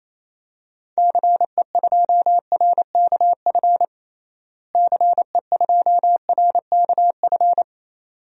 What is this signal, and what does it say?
Morse code, heard as a keyed tone, about 28 wpm, 700 Hz. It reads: CE2RKF CE2RKF
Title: DXCC-100 Call Signs - 6 Characters - Repeated 2x - 28wpm